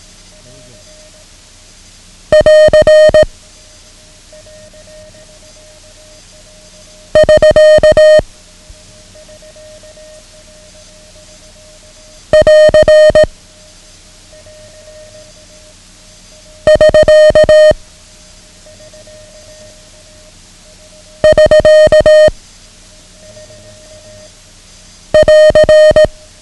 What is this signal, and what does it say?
Morse code, heard as a keyed tone, about 18 wpm, 620 Hz. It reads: <AR> <SK> <AR> <SK> <SK> <AR>
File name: ham radio / CodeQuick / Code Quick 09